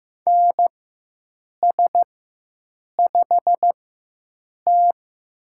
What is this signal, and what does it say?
Morse code, heard as a keyed tone, about 15 wpm, spaced as for 7 wpm, 700 Hz. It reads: NS5T
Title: USA Call Signs - 4 Characters - 4x Character Spacing - 15wpm